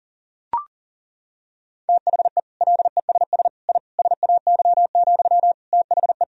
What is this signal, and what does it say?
Morse code, heard as a keyed tone, about 40 wpm, 700 Hz. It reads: THE LESS I SAY, THE
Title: Sentences from Top 1000 Words - Review Part 2 - 40wpm